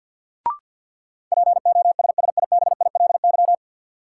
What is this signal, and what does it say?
Morse code, heard as a keyed tone, about 50 wpm, 700 Hz. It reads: POSSIBILY
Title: Sentences from Top 1300 Words - New Word Drill Alt - 50wpm